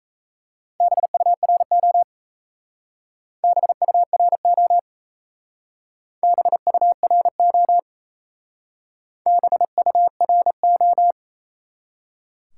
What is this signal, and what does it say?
Morse code, heard as a keyed tone, about 36 wpm, 700 Hz. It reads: BURO BURO BURO BURO